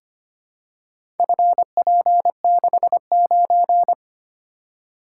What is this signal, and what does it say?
Morse code, heard as a keyed tone, about 25 wpm, 700 Hz. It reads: FP69
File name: practice / 25wpm / Four Letter-Number Combinations - 25wpm